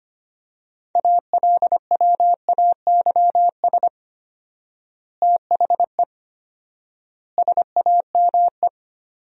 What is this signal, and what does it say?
Morse code, heard as a keyed tone, about 25 wpm, 700 Hz. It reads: ALWAYS THE SAME
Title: 4x Word-Spacing - Sentences from Top 300 Words - 25wpm